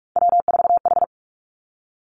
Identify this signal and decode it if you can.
Morse code, heard as a keyed tone, about 46 wpm, 700 Hz. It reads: W4H
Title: Three Challenging Letter-Number Combinations 45wpm